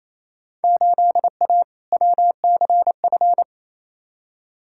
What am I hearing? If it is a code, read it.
Morse code, heard as a keyed tone, about 28 wpm, 700 Hz. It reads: 8A WCF
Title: Field Day Exchanges Encore - 28wpm